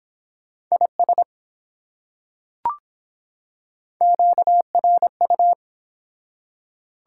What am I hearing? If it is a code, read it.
Morse code, heard as a keyed tone, about 23 wpm, 700 Hz. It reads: IS QRU